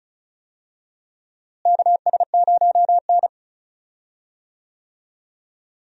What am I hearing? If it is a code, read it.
Morse code, heard as a keyed tone, about 36 wpm, 700 Hz. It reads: KS0N
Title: USA Call Signs - 4 Characters - Repeated 2x - 35wpm